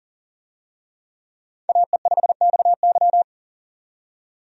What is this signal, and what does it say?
Morse code, heard as a keyed tone, about 40 wpm, 700 Hz. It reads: AE5XY
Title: USA Call Signs - 5 Characters - 40wpm